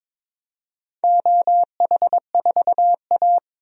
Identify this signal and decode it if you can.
Morse code, heard as a keyed tone, about 22 wpm, 700 Hz. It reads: OH4A
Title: DXCC-100 Call Signs - 4 Characters - 22wpm